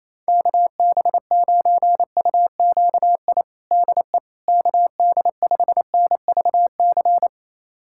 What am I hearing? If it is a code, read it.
Morse code, heard as a keyed tone, about 28 wpm, 700 Hz. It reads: KB9UQI DE KD5NVC
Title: USA Callsign DE Callsign - 6 Characters - 28wpm